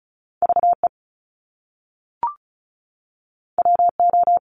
Morse code, heard as a keyed tone, about 36 wpm, 700 Hz. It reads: VE WO
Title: Top Two Letter Combinations in Words 35wpm